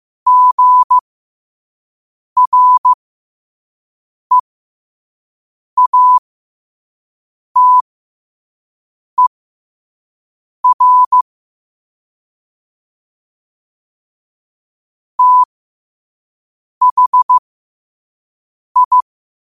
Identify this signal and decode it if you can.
Morse code, heard as a keyed tone, about 15 wpm, spaced as for 5 wpm, 1000 Hz. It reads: GREATER THI